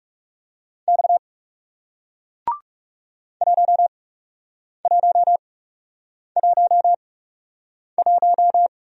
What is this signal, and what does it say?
Morse code, heard as a keyed tone, about 37 wpm, spaced as for 8 wpm, 700 Hz. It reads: X 1111
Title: Speed-Racing Single Letter-Number 30wpm